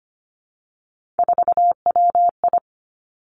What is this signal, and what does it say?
Morse code, heard as a keyed tone, about 25 wpm, 700 Hz. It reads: 4WI
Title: Three Challenging Letter-Number Combinations 25wpm